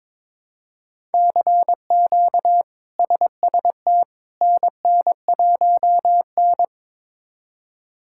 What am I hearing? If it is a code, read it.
Morse code, heard as a keyed tone, about 22 wpm, 700 Hz. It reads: CQ SST NN1N
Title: K1USN SST QSO Practice - 22wpm